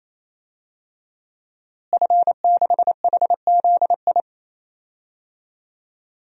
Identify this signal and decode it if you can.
Morse code, heard as a keyed tone, about 28 wpm, 700 Hz. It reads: F6HZI
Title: DXCC-100 Call Signs - 5 Characters - Repeated 3x - 28wpm